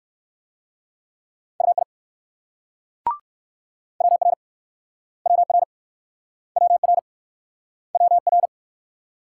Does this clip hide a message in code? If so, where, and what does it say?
Yes: Morse code from 1.6 to 8.5 s, heard as a keyed tone, about 59 wpm, 690 Hz. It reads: UI WR WR WR WR